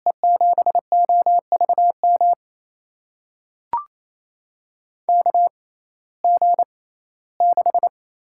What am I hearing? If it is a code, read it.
Morse code, heard as a keyed tone, about 28 wpm, spaced as for 17 wpm, 700 Hz. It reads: E7OVM KG6
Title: USA Call Signs - 6 Characters - 6x Character Spacing - 28wpm